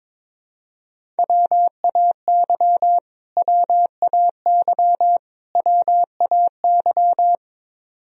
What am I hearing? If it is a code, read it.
Morse code, heard as a keyed tone, about 22 wpm, 700 Hz. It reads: WAY WAY WAY
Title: Top 100 Words Repeated 22wpm